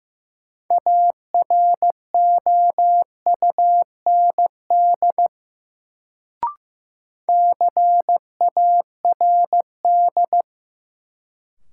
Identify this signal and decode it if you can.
Morse code, heard as a keyed tone, about 15 wpm, 700 Hz. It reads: AROUND CARD
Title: Sentences from Top 600 Words - New Word Drill - 15wpm